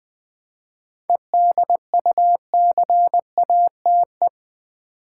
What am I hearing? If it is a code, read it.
Morse code, heard as a keyed tone, about 20 wpm, 700 Hz. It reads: EDUCATE
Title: 7 Letter Words - First 3 Letters Given - 20wpm